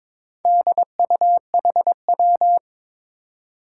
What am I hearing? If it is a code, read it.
Morse code, heard as a keyed tone, about 22 wpm, 700 Hz. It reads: DUHW
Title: Four Challenging Letter-Number Combinations 22wpm